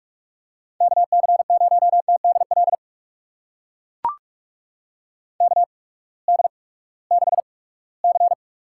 Morse code, heard as a keyed tone, about 41 wpm, spaced as for 22 wpm, 700 Hz. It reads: KC0TDL KD6C
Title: USA Call Signs - 8x Character Spacing - 45wpm